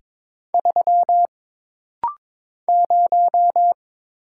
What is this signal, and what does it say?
Morse code, heard as a keyed tone, about 22 wpm, 700 Hz. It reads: 3 0
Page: This rapid-fire timing is geared to push your ICR (Instant Character Recognition) skills.